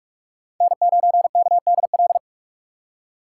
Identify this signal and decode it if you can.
Morse code, heard as a keyed tone, about 46 wpm, 700 Hz. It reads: N9KDL
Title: USA Call Signs - 5 Characters - 3x Character Spacing - 45wpm